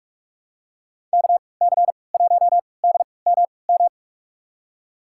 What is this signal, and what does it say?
Morse code, heard as a keyed tone, about 44 wpm, spaced as for 25 wpm, 690 Hz. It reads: KC1DMM
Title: USA Call Signs - 6 Characters - 3x Character Spacing - 45wpm